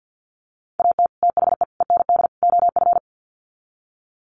Morse code, heard as a keyed tone, about 48 wpm, 700 Hz. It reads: AT THE END OF
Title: Sets of 4 Words 50wpm